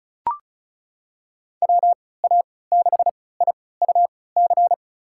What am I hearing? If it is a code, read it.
Morse code, heard as a keyed tone, about 34 wpm, spaced as for 20 wpm, 700 Hz. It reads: WA6IUC
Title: USA Call Signs - 6 Characters - 3x Character Spacing - 35wpm